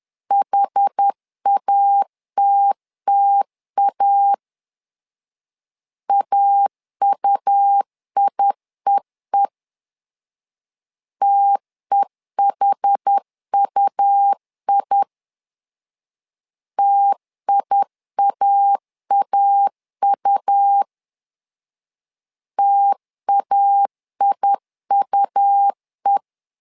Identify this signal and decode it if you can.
Morse code, heard as a keyed tone, about 10 wpm, 800 Hz. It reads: HATTA AUIEE TEHUI TIAAU TAIUE
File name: LEZIONI / 2 UHDN / 2 UH 10wpm 1714